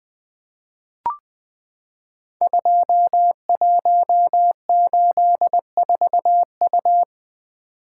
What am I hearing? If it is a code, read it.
Morse code, heard as a keyed tone, about 20 wpm, 700 Hz. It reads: 2184U
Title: Code Groups - 5 Letter:Numbers - 20wpm